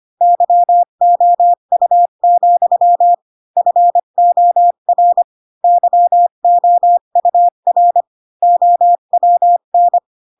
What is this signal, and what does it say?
Morse code, heard as a keyed tone, about 25 wpm, 700 Hz. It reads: YOU, FOR YOUR OWN
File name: Pinocchio / Pinocchio ch4 20 25 ik1hge